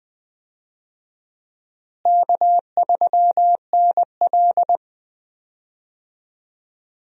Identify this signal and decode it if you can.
Morse code, heard as a keyed tone, about 20 wpm, 700 Hz. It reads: K3NL
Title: USA Call Signs - 4 Characters - Repeated 3x - 20wpm